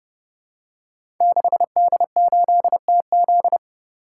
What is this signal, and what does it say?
Morse code, heard as a keyed tone, about 28 wpm, 700 Hz. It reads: 6D8TZ
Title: Code Groups - 5 Letter:Numbers - 30wpm